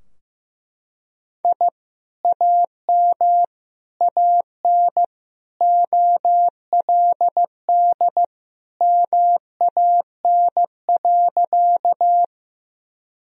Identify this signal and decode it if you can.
Morse code, heard as a keyed tone, about 15 wpm, 700 Hz. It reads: I AM AN OLD MAN.
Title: Sentences from Top 1000 Words - Review Part 2 - 15wpm